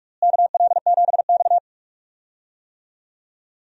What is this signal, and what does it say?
Morse code, heard as a keyed tone, about 46 wpm, 700 Hz. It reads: KL7X
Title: DXCC-100 Call Signs - 4 Characters - Repeated 2x - 45wpm